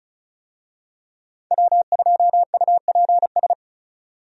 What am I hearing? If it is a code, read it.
Morse code, heard as a keyed tone, about 36 wpm, 700 Hz. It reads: W2UPS
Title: USA Call Signs - 5 Characters - 35wpm